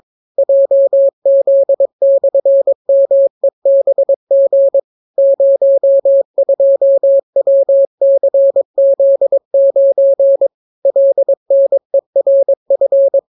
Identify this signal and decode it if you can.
Morse code, heard as a keyed tone, about 22 wpm, 550 Hz. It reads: JZ/MEBG 02WCZ9 LNERF